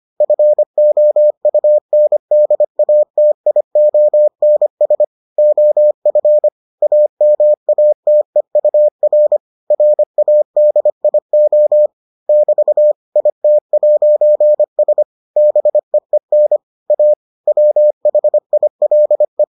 Morse code, heard as a keyed tone, about 25 wpm, 600 Hz. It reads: FOUNDATIONS OF AMATEUR RADIO <BT> IT'S BEEN A WHILE